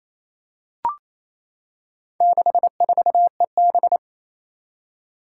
Morse code, heard as a keyed tone, about 28 wpm, 700 Hz. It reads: 64EB
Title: Four Challenging Letter-Number Combinations 28wpm